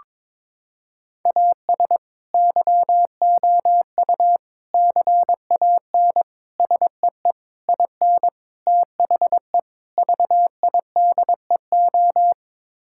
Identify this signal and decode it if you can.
Morse code, heard as a keyed tone, about 22 wpm, 700 Hz. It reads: AS YOU CAN SEE IN THE VIDEO
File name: practice / 22wpm / Sets of 7 Words - 22wpm